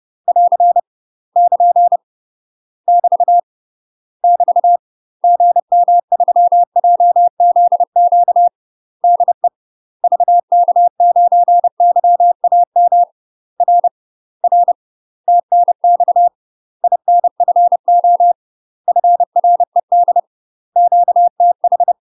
Morse code, heard as a keyed tone, about 30 wpm, 700 Hz. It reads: <AR> <KN> <BT> <BT> GM3JZQ DE VK9YAM R R TNX INFO FRED QTH
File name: UK MASTER QSOs TRAINING / 30WPM / UK MASTER QSOs 0027